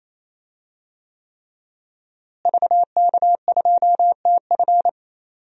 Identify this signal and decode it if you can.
Morse code, heard as a keyed tone, about 28 wpm, 700 Hz. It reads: VK2TF
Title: DXCC-100 Call Signs - 5 Characters - Repeated 3x - 28wpm